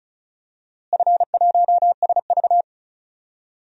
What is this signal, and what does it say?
Morse code, heard as a keyed tone, about 34 wpm, 700 Hz. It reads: F1SV
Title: DXCC-100 Call Signs - 4 Characters - Repeated 3x - 35wpm